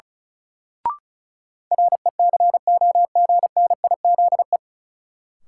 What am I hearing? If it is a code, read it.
Morse code, heard as a keyed tone, about 36 wpm, 700 Hz. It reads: RECOGNIZE
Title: Sentences from Top 1400 Words - New Word Drill - 35wpm